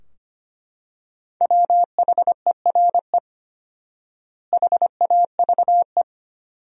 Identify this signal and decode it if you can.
Morse code, heard as a keyed tone, about 25 wpm, 700 Hz. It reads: WHERE HAVE
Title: 4x Word-Spacing - Sentences from Top 200 Words - 25wpm